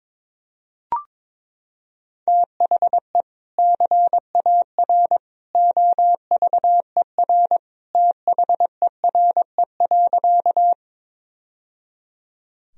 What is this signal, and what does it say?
Morse code, heard as a keyed tone, about 22 wpm, 700 Hz. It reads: THE CAR OVER THERE.
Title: Sentences from Top 1000 Words - Review Part 1 - 22wpm